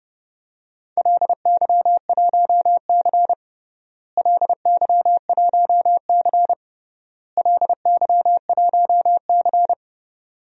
Morse code, heard as a keyed tone, about 29 wpm, 700 Hz. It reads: LY1C LY1C LY1C
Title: DXCC-100 Call Signs - 4 Characters - Repeated 3x - 30wpm